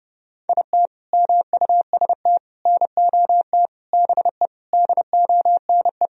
Morse code, heard as a keyed tone, about 30 wpm, 700 Hz. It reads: IT MUST NOT BE DONE
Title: Sentences from Top 1000 Words - Review Part 1 - 30wpm